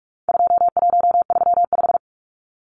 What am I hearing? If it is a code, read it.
Morse code, heard as a keyed tone, about 44 wpm, 700 Hz. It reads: 2135
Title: Numbers - 4-Digit 45wpm